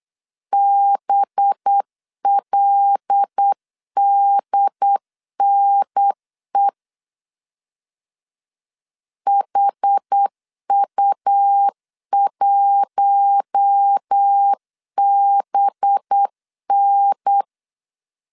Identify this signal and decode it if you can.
Morse code, heard as a keyed tone, about 8 wpm, 800 Hz. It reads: BLDNE HU1BN
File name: 8wpm